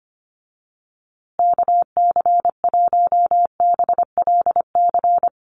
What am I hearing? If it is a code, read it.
Morse code, heard as a keyed tone, about 25 wpm, 700 Hz. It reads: KC1BLC
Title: USA Call Signs Repeated 3x 25wpm